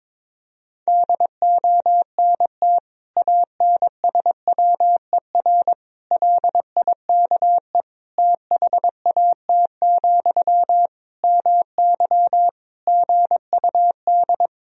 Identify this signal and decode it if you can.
Morse code, heard as a keyed tone, about 22 wpm, 700 Hz. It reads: DONT ANSWER LIKE THAT, MY GUD
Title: Sentences from Top 400 Words and Abbreviations 22wpm